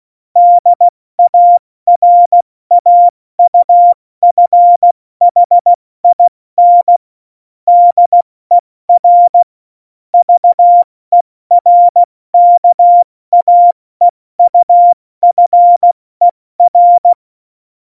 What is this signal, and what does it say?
Morse code, heard as a keyed tone, about 16 wpm, 700 Hz. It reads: DARAUFHIN DER VERKAEUFER
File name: Morsekurs / Witze / Witz 15. Januar 2019, 70 F80